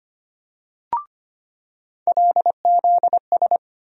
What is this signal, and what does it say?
Morse code, heard as a keyed tone, about 25 wpm, 700 Hz. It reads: LZS
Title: USA Call Sign Postfixes - 3 Letters 25wpm